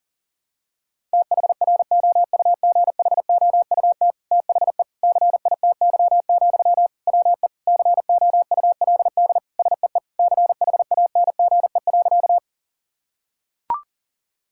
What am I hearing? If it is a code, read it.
Morse code, heard as a keyed tone, about 40 wpm, 700 Hz. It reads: THROUGHOUT THE CITY, WE COULD SEE CHANGE.
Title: Sentences from Top 1000 Words - Review Part 5 - 40wpm